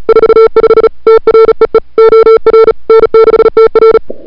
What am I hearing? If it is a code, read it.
Morse code, heard as a keyed tone, about 36 wpm, 440 Hz. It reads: 45 TREE OR N6TR
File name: n6tr / Fab2007CWSprintHour1